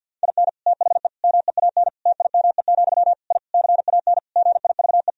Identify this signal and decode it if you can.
Morse code, heard as a keyed tone, about 50 wpm, 700 Hz. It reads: IN THE MEAN TIME, I CAN GIVE
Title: Sentences from Top 1000 Words - Review Part 3 - 50wpm